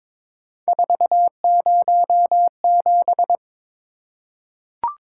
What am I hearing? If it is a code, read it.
Morse code, heard as a keyed tone, about 22 wpm, 700 Hz. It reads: 407
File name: practice / 22wpm / Three Letter-Number Combinations - 22wpm